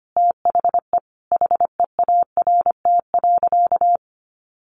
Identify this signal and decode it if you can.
Morse code, heard as a keyed tone, about 25 wpm, 700 Hz. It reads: THE HEART.
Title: Sentences from Top 1200 Words 25wpm